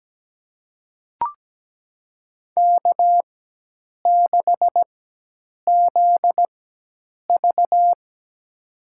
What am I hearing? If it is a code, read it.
Morse code, heard as a keyed tone, about 17 wpm, spaced as for 8 wpm, 700 Hz. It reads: K6ZV
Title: USA Call Signs - 4 Characters - 4x Character Spacing - 17wpm